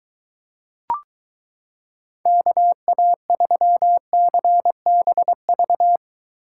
Morse code, heard as a keyed tone, about 23 wpm, 700 Hz. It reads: KA3CBV